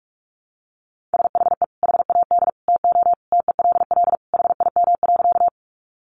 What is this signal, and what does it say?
Morse code, heard as a keyed tone, about 44 wpm, 700 Hz. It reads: SHE HAD TO TELL HIM.
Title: Sentences from Top 200 Words 45wpm